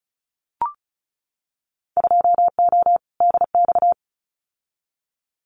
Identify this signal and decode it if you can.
Morse code, heard as a keyed tone, about 34 wpm, 700 Hz. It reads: 2O DX
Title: Winter Field Day Exchanges Encore 35wpm